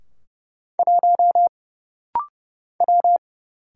Morse code, heard as a keyed tone, about 30 wpm, 700 Hz. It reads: 1 W
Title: Single Letter-Number - Rapid-Fire 30wpm